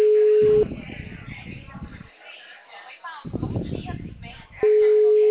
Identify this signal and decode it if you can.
Ringback tone (Europe, 425 Hz), with speech.
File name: out- -2021- - - .3495